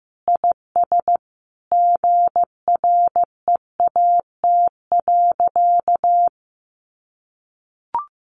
Morse code, heard as a keyed tone, about 15 wpm, 700 Hz. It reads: IS GREAT.
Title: Sentences from Top 600 Words 15wpm